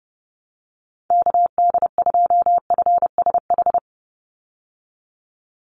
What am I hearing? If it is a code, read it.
Morse code, heard as a keyed tone, about 30 wpm, 700 Hz. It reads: KD2FSH
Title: Call Signs Repeated 2x - 30wpm